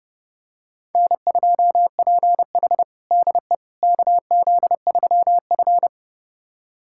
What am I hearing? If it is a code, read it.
Morse code, heard as a keyed tone, about 29 wpm, 700 Hz. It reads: N2PH DE KZ3F
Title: USA Callsign DE Callsign - 4 Characters - 30wpm